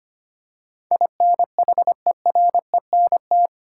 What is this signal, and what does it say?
Morse code, heard as a keyed tone, about 25 wpm, 700 Hz. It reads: INHERENT
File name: 8 Letter Words - First 3 Letters Given - 25wpm